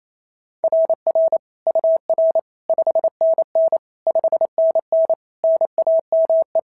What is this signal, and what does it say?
Morse code, heard as a keyed tone, about 27 wpm, 690 Hz. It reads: RR UR 5NN 5NN NAME
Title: SOTA QSO Practice 28wpm